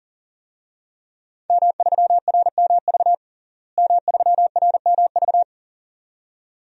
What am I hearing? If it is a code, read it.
Morse code, heard as a keyed tone, about 40 wpm, 700 Hz. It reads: M3RMV M3RMV
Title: DXCC-100 Call Signs - 5 Characters - Repeated 2x - 40wpm